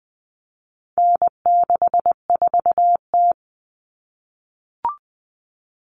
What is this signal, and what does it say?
Morse code, heard as a keyed tone, about 20 wpm, 700 Hz. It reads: N64T
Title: Common Cut Numbers - 4-Digit 20wpm